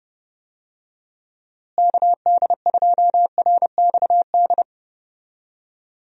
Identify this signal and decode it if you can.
Morse code, heard as a keyed tone, about 30 wpm, 700 Hz. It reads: KD2RXD
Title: USA Call Signs - 6 Characters - Repeated 2x - 30wpm